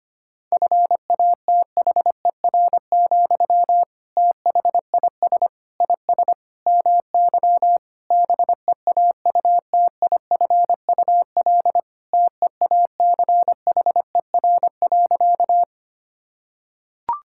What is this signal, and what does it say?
Morse code, heard as a keyed tone, about 25 wpm, 700 Hz. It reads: FATHER, THIS IS MY BEAUTIFUL TEACHER.
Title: Sentences from Top 1000 Words - Review Part 6 - 25wpm